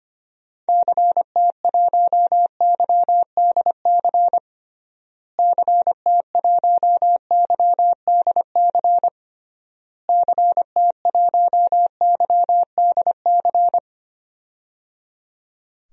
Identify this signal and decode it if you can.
Morse code, heard as a keyed tone, about 25 wpm, 700 Hz. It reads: CT1YDC CT1YDC CT1YDC
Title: DXCC-100 Call Signs - 6 Characters - Repeated 3x - 25wpm